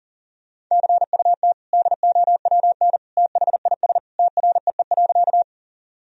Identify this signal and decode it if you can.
Morse code, heard as a keyed tone, about 37 wpm, 700 Hz. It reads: CUT DOWN THIS TREE.
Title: Sentences from Top 1000 Words - Review Part 1 - 40wpm